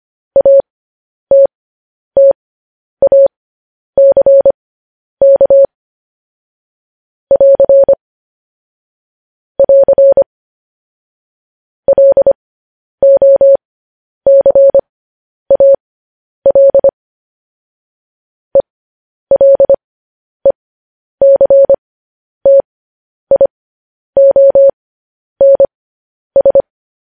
Morse code, heard as a keyed tone, about 25 wpm, spaced as for 10 wpm, 550 Hz. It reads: ATTACK <AR> <AR> LOCAL ELECTIONS
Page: news-headlines-in-morse-code-at-10-wpm